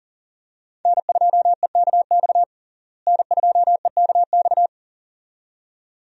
Morse code, heard as a keyed tone, about 40 wpm, 700 Hz. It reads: N2EKX N2EKX
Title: USA Call Signs - 5 Characters - Repeated 2x - 40wpm